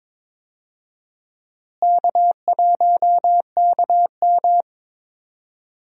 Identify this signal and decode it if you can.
Morse code, heard as a keyed tone, about 22 wpm, 700 Hz. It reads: K1KM
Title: USA Call Signs - 4 Characters - Repeated 2x - 22wpm